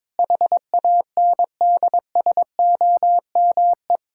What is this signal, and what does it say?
Morse code, heard as a keyed tone, about 22 wpm, 700 Hz. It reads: HANDSOME
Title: Pangram Challenge v2 - 22wpm